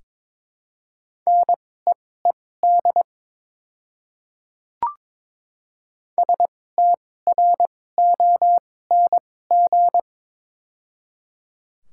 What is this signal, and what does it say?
Morse code, heard as a keyed tone, about 22 wpm, spaced as for 16 wpm, 700 Hz. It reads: NEED STRONG
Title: 2x Character Spacing - Top 500 Words - 22wpm